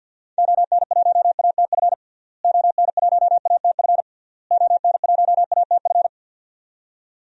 Morse code, heard as a keyed tone, about 52 wpm, 700 Hz. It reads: ON1ATF ON1ATF ON1ATF
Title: DXCC-100 Call Signs - 6 Characters - Repeated 3x - 50wpm